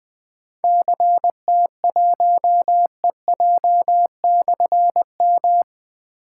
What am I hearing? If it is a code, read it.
Morse code, heard as a keyed tone, about 20 wpm, 700 Hz. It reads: CT1EJ/M
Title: Difficult Call Signs 20wpm